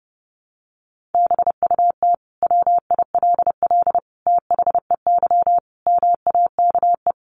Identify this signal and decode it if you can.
Morse code, heard as a keyed tone, about 30 wpm, 700 Hz. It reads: BUT WILL THEY MAKE
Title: Sentences from Top 1000 Words 30wpm